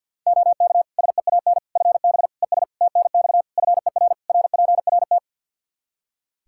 Morse code, heard as a keyed tone, about 51 wpm, 700 Hz. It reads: OK SEAN FB ES TNX FER RPRT
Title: Practice QSO - Exchange 2 - Info - 50wpm